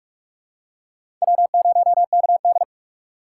Morse code, heard as a keyed tone, about 46 wpm, 700 Hz. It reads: W0KD
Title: USA Call Signs - 4 Characters - 4x Character Spacing - 45wpm